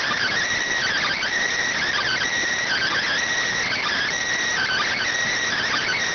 Emergency Alert System (SAME) tones, heard as harsh buzzing bursts. It is heard from 1.0 to 5.0 s.